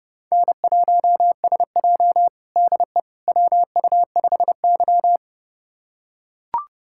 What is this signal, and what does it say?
Morse code, heard as a keyed tone, about 30 wpm, 700 Hz. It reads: N1SJ DE WU5Y
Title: USA Callsign DE Callsign - 4 Characters - 30wpm